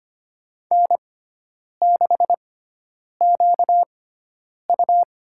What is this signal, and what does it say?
Morse code, heard as a keyed tone, about 25 wpm, spaced as for 9 wpm, 700 Hz. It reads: N6QU
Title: USA Call Signs - 4 Characters - 6x Character Spacing - 25wpm